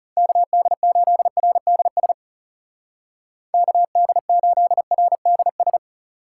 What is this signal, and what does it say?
Morse code, heard as a keyed tone, about 36 wpm, 700 Hz. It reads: KD8RDS KD8RDS
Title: Speed Racing - USA 2x3 Call Signs 30wpm